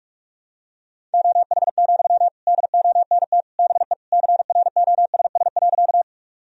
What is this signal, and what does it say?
Morse code, heard as a keyed tone, about 43 wpm, 700 Hz. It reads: OH, DONT BE CROSS.
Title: Sentences from Top 1000 Words - Review Part 1 - 45wpm